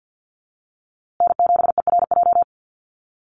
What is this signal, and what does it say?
Morse code, heard as a keyed tone, about 50 wpm, 700 Hz. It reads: N7ERJ
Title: USA Call Signs Repeated 2x 50wpm